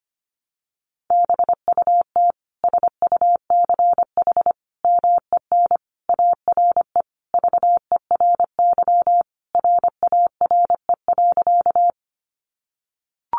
Morse code, heard as a keyed tone, about 25 wpm, 700 Hz. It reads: BUT SUCH MEN ARE VERY RARE.
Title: Sentences from Top 1100 Words 25wpm